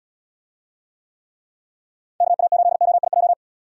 Morse code, heard as a keyed tone, about 75 wpm, 700 Hz. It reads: KA8OIJ